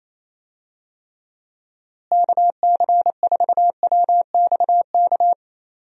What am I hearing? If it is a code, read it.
Morse code, heard as a keyed tone, about 28 wpm, 700 Hz. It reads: KC4WXK
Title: USA Call Signs - 6 Characters - Repeated 3x - 28wpm